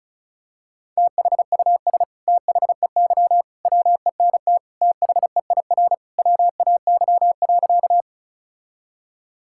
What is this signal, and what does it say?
Morse code, heard as a keyed tone, about 36 wpm, 700 Hz. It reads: THUS THEY WENT THEIR WAY.
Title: Sentences from Top 1000 Words - Review Part 5 - 35wpm